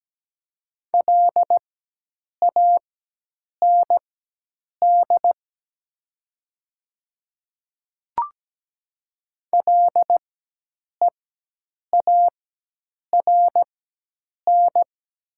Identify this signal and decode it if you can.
Morse code, heard as a keyed tone, about 17 wpm, spaced as for 8 wpm, 690 Hz. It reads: LAND LEARN
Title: 4x Character Spacing - Top 300 Words - 17wpm